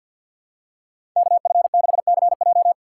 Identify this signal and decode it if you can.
Morse code, heard as a keyed tone, about 51 wpm, 700 Hz. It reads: KF6CJ
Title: USA Call Signs - 5 Characters - 4x Character Spacing - 50wpm